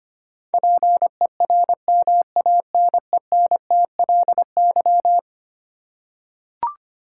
Morse code, heard as a keyed tone, about 25 wpm, 700 Hz. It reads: PERMANENTLY
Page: morse-code-ninja-practice-25wpm